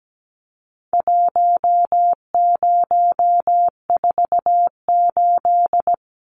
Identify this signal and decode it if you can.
Morse code, heard as a keyed tone, about 17 wpm, 700 Hz. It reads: 1048
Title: Numbers - Contest Style - 4-Digit 17wpm